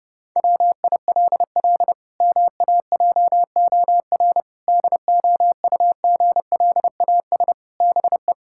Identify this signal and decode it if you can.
Morse code, heard as a keyed tone, about 30 wpm, 700 Hz. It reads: WILL MAJOR DOUGLAS BE
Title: Pangram Challenge v2 - 30wpm